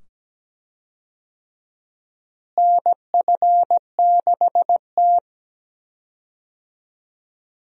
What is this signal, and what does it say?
Morse code, heard as a keyed tone, about 17 wpm, 700 Hz. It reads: NF6T
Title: USA Call Signs - 4 Characters - Repeated 2x - 17wpm